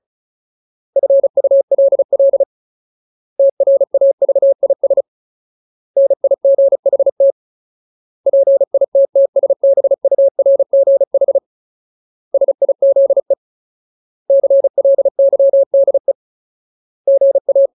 Morse code, heard as a keyed tone, about 34 wpm, 550 Hz. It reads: FULL TRAVIS NIGHT PITTSBURGH SIZE CLYDE GA
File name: Ultimate Word List / 35WPM Ultimate